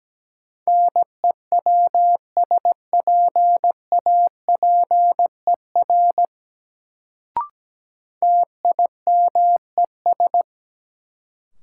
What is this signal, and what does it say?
Morse code, heard as a keyed tone, about 17 wpm, 700 Hz. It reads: NEWSPAPER TIMES
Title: Sentences from Top 1100 Words - New Word Drill Alt - 17wpm